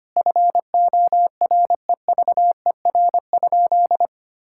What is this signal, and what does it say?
Morse code, heard as a keyed tone, about 25 wpm, 700 Hz. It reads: FOREVER?
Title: Sentences from Top 1600 Words 25wpm